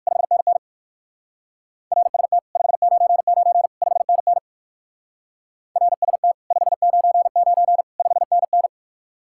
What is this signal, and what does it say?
Morse code, heard as a keyed tone, about 55 wpm, 700 Hz. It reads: 5NN RST 599 5NN RST 599 5NN
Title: Speed-Racing Common QSO Phrases - 45wpm